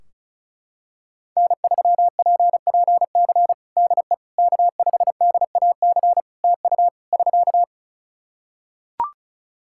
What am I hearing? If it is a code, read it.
Morse code, heard as a keyed tone, about 36 wpm, 700 Hz. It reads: N3PPC DE K5DAC TU <SK>
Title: K1USN SST QSO Practice 35wpm